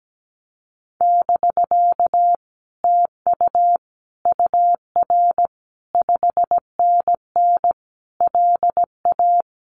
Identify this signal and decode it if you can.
Morse code, heard as a keyed tone, about 17 wpm, 700 Hz. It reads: <BK> TU UR 5NN LA
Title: Common QSO Phrases Encore 17wpm